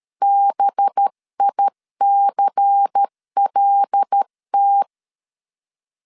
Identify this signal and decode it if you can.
Morse code, heard as a keyed tone, about 13 wpm, 800 Hz. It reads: BICLT
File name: CW / DET / 1 6 1500C S C 12wpm 3533